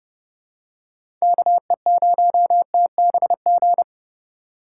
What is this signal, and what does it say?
Morse code, heard as a keyed tone, about 30 wpm, 700 Hz. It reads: KE0TBG
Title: USA Call Signs - 6 Characters - 2x Character Spacing - 30wpm